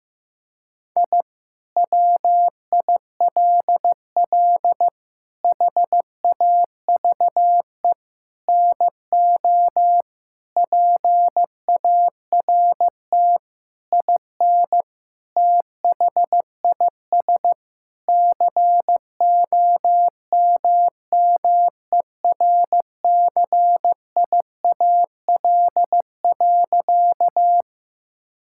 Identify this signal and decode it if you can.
Morse code, heard as a keyed tone, about 16 wpm, 700 Hz. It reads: I WILL HAVE NO PART IN THIS COMMERCIAL.
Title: Sentences from Top 1000 Words - Review Part 5 - 15wpm